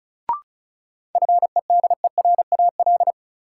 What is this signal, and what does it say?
Morse code, heard as a keyed tone, about 36 wpm, 700 Hz. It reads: FEDERAL
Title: Sets of 3 Words - New Word Drill - 35wpm